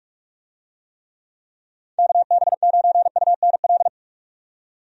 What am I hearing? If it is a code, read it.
Morse code, heard as a keyed tone, about 45 wpm, 700 Hz. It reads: KB9UNL
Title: USA Call Signs - 6 Characters - 45wpm